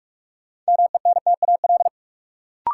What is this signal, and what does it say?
Morse code, heard as a keyed tone, about 44 wpm, 700 Hz. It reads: MENTAL
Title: Sentences from Top 1000 Words - Review Part 4 - New Word Drill - 45wpm